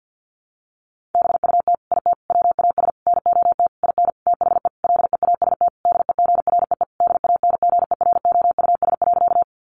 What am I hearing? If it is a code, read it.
Morse code, heard as a keyed tone, about 50 wpm, 700 Hz. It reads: BUT IT WAS NOT IN THE LEAST DEGREE DANGEROUS.
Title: Sentences from Top 1500 Words 50wpm